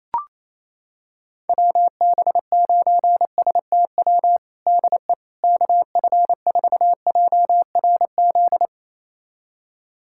Morse code, heard as a keyed tone, about 27 wpm, 700 Hz. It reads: WB9STW DE KF4JRZ
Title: USA Callsign DE Callsign 28wpm